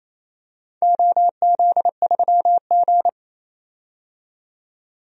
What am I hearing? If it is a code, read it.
Morse code, heard as a keyed tone, about 28 wpm, 700 Hz. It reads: OZ3G
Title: DXCC-100 Call Signs - 4 Characters - 28wpm